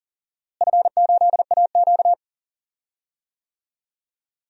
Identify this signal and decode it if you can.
Morse code, heard as a keyed tone, about 40 wpm, 700 Hz. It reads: F8AQ
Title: DXCC-100 Call Signs - 4 Characters - Repeated 2x - 40wpm